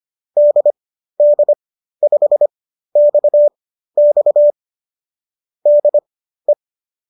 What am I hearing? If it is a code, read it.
Morse code, heard as a keyed tone, about 25 wpm, spaced as for 13 wpm, 570 Hz. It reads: DD5XX DE